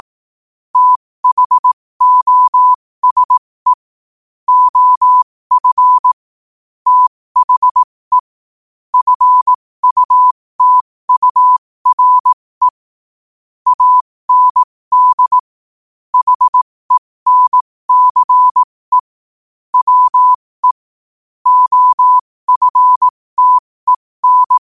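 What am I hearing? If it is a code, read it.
Morse code, heard as a keyed tone, about 18 wpm, spaced as for 15 wpm, 1000 Hz. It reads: THOSE OF THE FUTURE AND HENCE WE OFTEN
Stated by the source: Morse code quote of the day sent at 15 words per minute.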